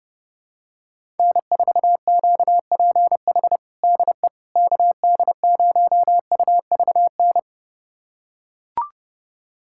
Morse code, heard as a keyed tone, about 32 wpm, 700 Hz. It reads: N4QPH DE KD0UVN
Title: USA Callsign DE Callsign 30wpm